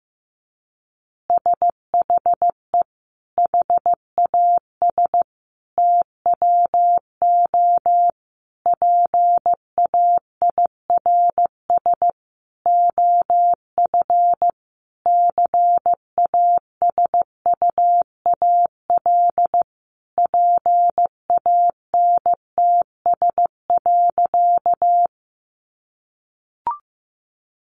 Morse code, heard as a keyed tone, about 15 wpm, 700 Hz. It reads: SHE HAS TWO PAIRS OF CASUAL PANTS.
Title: Additional Easy Phrases 15wpm